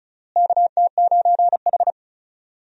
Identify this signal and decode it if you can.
Morse code, heard as a keyed tone, about 34 wpm, 700 Hz. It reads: KT9H
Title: USA Call Signs - 4 Characters - Repeated 3x - 35wpm